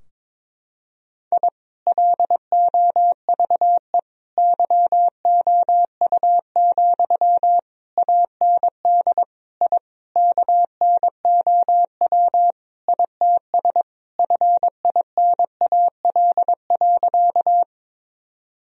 Morse code, heard as a keyed tone, about 21 wpm, 700 Hz. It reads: I LOVE YOU, AND I KNOW ITS FINAL.
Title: Sentences from Top 1000 Words - Review Part 2 - 22wpm